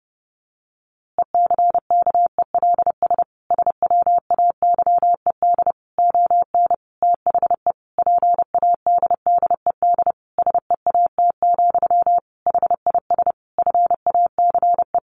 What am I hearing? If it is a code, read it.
Morse code, heard as a keyed tone, about 28 wpm, 700 Hz. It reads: ECKELS SWAYED ON THE PADDED SEAT, HIS FACE
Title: A Sound Of Thunder 30wpm